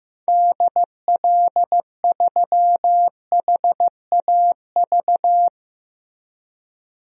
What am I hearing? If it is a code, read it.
Morse code, heard as a keyed tone, about 15 wpm, 700 Hz. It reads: DL3HAV
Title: DXCC-100 Call Signs - 6 Characters - Repeated 2x - 15wpm